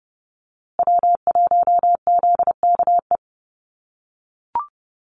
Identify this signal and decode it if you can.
Morse code, heard as a keyed tone, about 30 wpm, 700 Hz. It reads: W1ZKE
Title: USA Call Signs Repeated 3x 30wpm